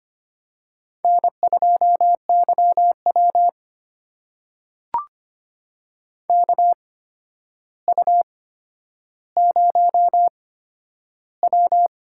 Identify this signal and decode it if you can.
Morse code, heard as a keyed tone, about 25 wpm, spaced as for 11 wpm, 700 Hz. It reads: N2YW KU0W
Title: USA Call Signs - 4 Characters - 8x Character Spacing - 25wpm